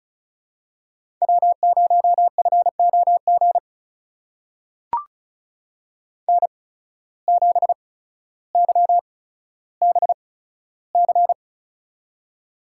Morse code, heard as a keyed tone, about 36 wpm, spaced as for 15 wpm, 700 Hz. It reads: W0FOG N7YBC
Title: USA Call Signs - 5 Characters - 8x Character Spacing - 35wpm